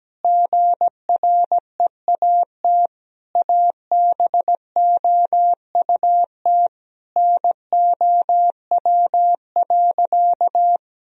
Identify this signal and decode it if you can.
Morse code, heard as a keyed tone, about 17 wpm, 700 Hz. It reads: GREAT ABOUT NOW.
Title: Sentences from Top 1000 Words - Review Part 1 - 17wpm